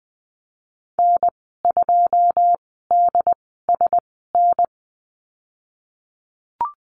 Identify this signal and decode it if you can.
Morse code, heard as a keyed tone, about 20 wpm, spaced as for 15 wpm, 700 Hz. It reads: N2DSN
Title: 154 Lesson 25g - Letter p - Call Signs 20wpm